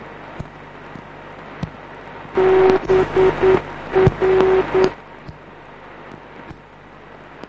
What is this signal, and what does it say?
Morse code, heard as a keyed tone, about 9 wpm, 380 Hz. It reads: BR